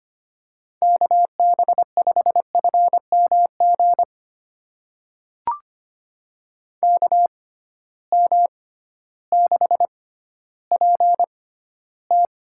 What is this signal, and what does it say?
Morse code, heard as a keyed tone, about 25 wpm, spaced as for 13 wpm, 700 Hz. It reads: KB5FMG KM6PT
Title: USA Call Signs - 6 Characters - 6x Character Spacing - 25wpm